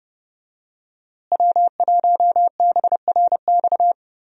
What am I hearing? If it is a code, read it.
Morse code, heard as a keyed tone, about 30 wpm, 700 Hz. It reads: W1BRX
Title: USA Call Signs - 5 Characters - Repeated 3x - 30wpm